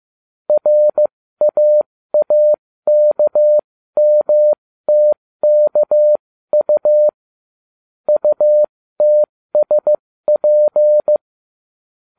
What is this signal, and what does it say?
Morse code, heard as a keyed tone, about 15 wpm, spaced as for 13 wpm, 600 Hz. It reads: RAAKMTKU UTSP